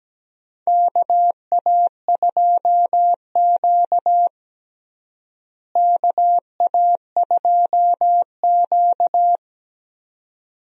Morse code, heard as a keyed tone, about 17 wpm, 700 Hz. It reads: KA2Q KA2Q
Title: USA Call Signs - 4 Characters - Repeated 2x - 17wpm